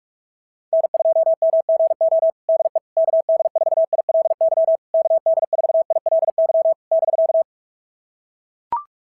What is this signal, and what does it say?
Morse code, heard as a keyed tone, about 44 wpm, 690 Hz. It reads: N2MGO DE KD4ILY KD4ILY <BK>
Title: SOTA QSO Practice 45wpm